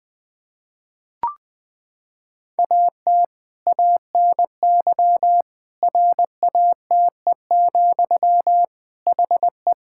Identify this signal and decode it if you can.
Morse code, heard as a keyed tone, about 20 wpm, 700 Hz. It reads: AT ANY RATE, HE
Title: Sentences from Top 1000 Words - Review Part 5 - 20wpm